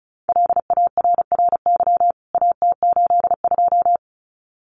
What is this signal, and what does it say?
Morse code, heard as a keyed tone, about 34 wpm, 700 Hz. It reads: LARRY AT82
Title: CWT Exchange 35wpm